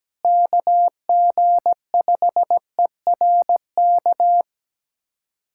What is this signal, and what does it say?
Morse code, heard as a keyed tone, about 17 wpm, 700 Hz. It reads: KG5ERK
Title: USA Callsign DE Callsign - 6 Characters - 17wpm